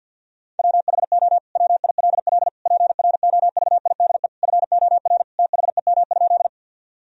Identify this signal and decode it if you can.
Morse code, heard as a keyed tone, about 50 wpm, 700 Hz. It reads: WHO WILL PROVIDE FOR THEM?
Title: Sentences from Top 1000 Words - Review Part 3 - 50wpm